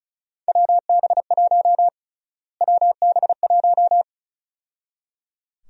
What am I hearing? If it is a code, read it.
Morse code, heard as a keyed tone, about 36 wpm, 700 Hz. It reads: WB1 WB1
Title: USA Call Sign Prefixes - Repeated 2x - 35wpm